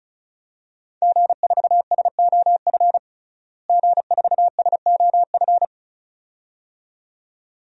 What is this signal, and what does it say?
Morse code, heard as a keyed tone, about 34 wpm, 700 Hz. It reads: G4SOF G4SOF
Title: DXCC-100 Call Signs - 5 Characters - Repeated 2x - 35wpm